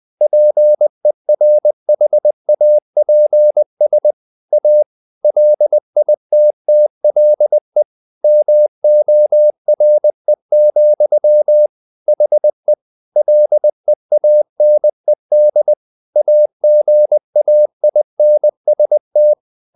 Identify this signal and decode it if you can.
Morse code, heard as a keyed tone, about 20 wpm, 600 Hz. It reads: PERHAPS A LITTLE MORE, HE LEANED AGAINST
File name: the old man and the sea / the old man and the sea part 1 20wpm